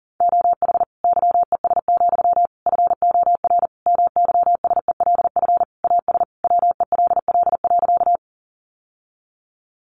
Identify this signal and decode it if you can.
Morse code, heard as a keyed tone, about 40 wpm, 700 Hz. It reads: OH YES, FOR MYSELF AS WELL.
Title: Sentences from Top 1300 Words 40wpm